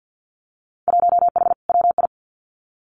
Morse code, heard as a keyed tone, about 50 wpm, 700 Hz. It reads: 1H WI
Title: Winter Field Day Exchanges 50wpm